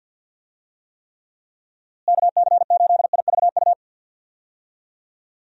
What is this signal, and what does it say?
Morse code, heard as a keyed tone, about 51 wpm, 700 Hz. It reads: KC8IVU
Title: USA Call Signs - 6 Characters - Repeated 3x - 50wpm